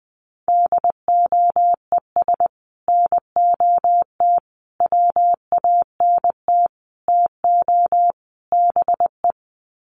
Morse code, heard as a keyed tone, about 20 wpm, 700 Hz. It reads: DOES NOT WANT TO BE
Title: Sets of 5 Words Encore 20wpm